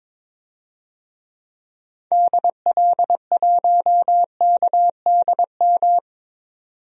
Morse code, heard as a keyed tone, about 22 wpm, 700 Hz. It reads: DL1KDM
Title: DXCC-100 Call Signs - 6 Characters - 22wpm